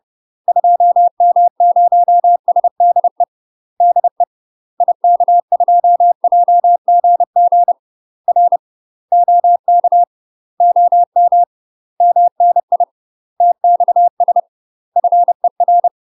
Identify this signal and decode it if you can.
Morse code, heard as a keyed tone, about 30 wpm, 700 Hz. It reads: 2M0SDE DE IK2JGG R OK OM MNI TXS FER
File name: UK MASTER QSOs TRAINING / 30WPM / UK MASTER QSOs 0001